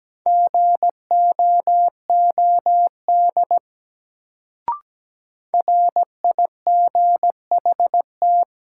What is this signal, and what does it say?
Morse code, heard as a keyed tone, about 17 wpm, 700 Hz. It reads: GOOD RIGHT
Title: Sentences from Top 200 Words - New Word Drill - 17wpm